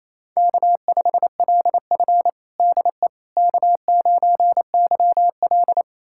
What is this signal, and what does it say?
Morse code, heard as a keyed tone, about 28 wpm, 700 Hz. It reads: K5LF DE K9YL
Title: USA Callsign DE Callsign - 4 Characters - 28wpm